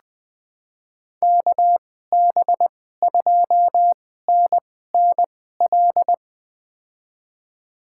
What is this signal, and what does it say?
Morse code, heard as a keyed tone, about 20 wpm, spaced as for 14 wpm, 700 Hz. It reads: KB2NNL
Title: USA Call Signs - 2x Character Spacing - 20wpm